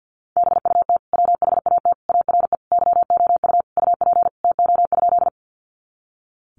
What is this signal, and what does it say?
Morse code, heard as a keyed tone, about 51 wpm, 700 Hz. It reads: BUT WHAT ARE YOU UP TO?
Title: Sentences from Top 100 Words 50wpm